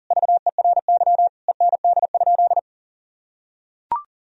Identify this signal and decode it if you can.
Morse code, heard as a keyed tone, about 37 wpm, 700 Hz. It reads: VERY END?
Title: Sentences from Top 1000 Words - Review Part 3 - 40wpm